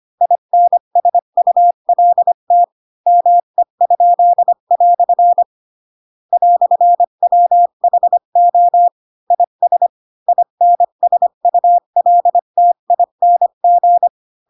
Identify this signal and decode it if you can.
Morse code, heard as a keyed tone, about 25 wpm, 700 Hz. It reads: INSULT ME?" "WHO IS INSULTING